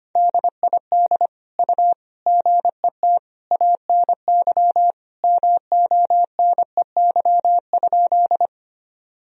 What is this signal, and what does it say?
Morse code, heard as a keyed tone, about 25 wpm, 700 Hz. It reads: DID U GET ANY MONEY?
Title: Sentences from Top 400 Words and Abbreviations 25wpm